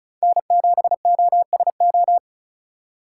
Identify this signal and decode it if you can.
Morse code, heard as a keyed tone, about 36 wpm, 690 Hz. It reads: N7OSO
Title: USA Call Signs - 5 Characters - 3x Character Spacing - 35wpm